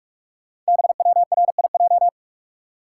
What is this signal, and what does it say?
Morse code, heard as a keyed tone, about 44 wpm, 700 Hz. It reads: BWRIJ
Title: Code Groups - 5 Letters - 45wpm